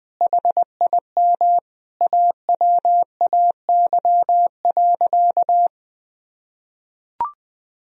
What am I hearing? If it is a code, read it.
Morse code, heard as a keyed tone, about 20 wpm, 700 Hz. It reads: HIM AWAY.
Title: Sentences from Top 1000 Words - Review Part 4 - 20wpm